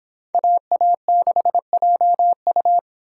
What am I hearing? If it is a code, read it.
Morse code, heard as a keyed tone, about 26 wpm, 700 Hz. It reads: AA6JU